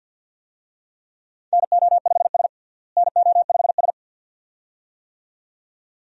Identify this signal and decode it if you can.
Morse code, heard as a keyed tone, about 47 wpm, 700 Hz. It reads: NO5S NO5S
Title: USA Call Signs - 4 Characters - Repeated 2x - 50wpm